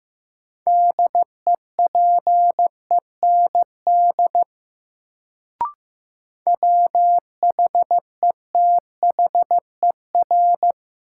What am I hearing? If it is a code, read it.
Morse code, heard as a keyed tone, about 15 wpm, 700 Hz. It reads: DEPEND WHETHER
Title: Sentences from Top 1300 Words - New Word Drill Alt - 15wpm